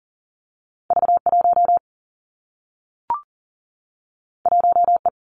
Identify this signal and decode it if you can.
Morse code, heard as a keyed tone, about 43 wpm, 700 Hz. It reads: V 1 1 E
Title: Two Challenging Letter-Number Combinations 40wpm